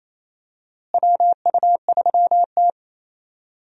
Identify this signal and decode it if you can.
Morse code, heard as a keyed tone, about 28 wpm, 700 Hz. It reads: WU3T
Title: USA Call Signs - 4 Characters - Repeated 2x - 28wpm